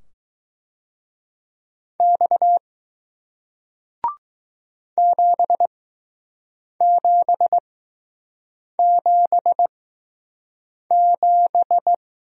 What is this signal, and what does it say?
Morse code, heard as a keyed tone, about 20 wpm, spaced as for 6 wpm, 700 Hz. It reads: X 7777